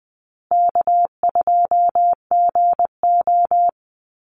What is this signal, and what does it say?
Morse code, heard as a keyed tone, about 20 wpm, 700 Hz. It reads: K2GO
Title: Call Signs Repeated 2x - 20wpm